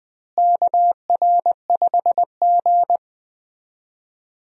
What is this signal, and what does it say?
Morse code, heard as a keyed tone, about 20 wpm, 700 Hz. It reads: KR5G
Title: USA Call Signs - 4 Characters - Repeated 2x - 20wpm